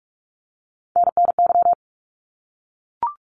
Morse code, heard as a keyed tone, about 46 wpm, 700 Hz. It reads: NNY
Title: ARRL Sections 45wpm